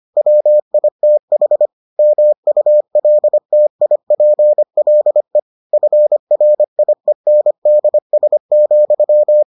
Morse code, heard as a keyed tone, about 25 wpm, 600 Hz. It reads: WITH MULTIPLE FRIENDS,